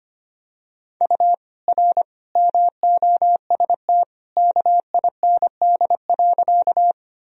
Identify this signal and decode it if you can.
Morse code, heard as a keyed tone, about 25 wpm, 700 Hz. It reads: U R MOST KIND.
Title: Sentences from Top 200 Words and Abbreviations 25wpm